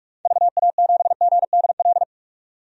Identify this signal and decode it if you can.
Morse code, heard as a keyed tone, about 42 wpm, 700 Hz. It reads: VA7GDL
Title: DXCC-100 Call Signs - 6 Characters - 45wpm